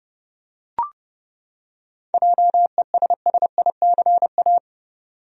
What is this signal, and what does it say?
Morse code, heard as a keyed tone, about 30 wpm, 700 Hz. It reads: JESSICA
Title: USA Common Names - 30wpm